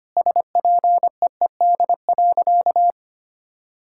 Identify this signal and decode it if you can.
Morse code, heard as a keyed tone, about 25 wpm, 700 Hz. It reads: SPEED.
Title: Sentences from Top 1000 Words - Review Part 5 - 25wpm